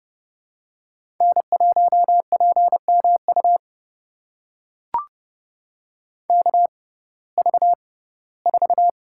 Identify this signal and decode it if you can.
Morse code, heard as a keyed tone, about 30 wpm, spaced as for 18 wpm, 700 Hz. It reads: N1PMU KV4
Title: USA Call Signs - 5 Characters - 6x Character Spacing - 30wpm